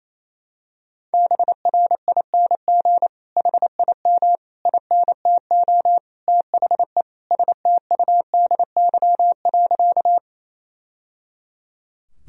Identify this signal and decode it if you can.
Morse code, heard as a keyed tone, about 27 wpm, 700 Hz. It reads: BRING HIM INTO THE STUDY.
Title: Sentences from Top 1000 Words - Review Part 1 - 28wpm